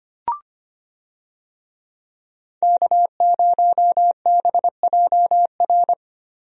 Morse code, heard as a keyed tone, about 25 wpm, 700 Hz. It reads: K0BJR
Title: USA Call Signs - 5 Characters - 25wpm